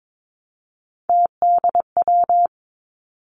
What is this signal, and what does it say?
Morse code, heard as a keyed tone, about 22 wpm, 700 Hz. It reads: TDW
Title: Three Challenging Letter-Number Combinations 22wpm